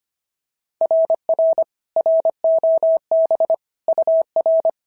Morse code, heard as a keyed tone, about 22 wpm, 650 Hz. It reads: RR ROB UR
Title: SOTA QSO Practice 25wpm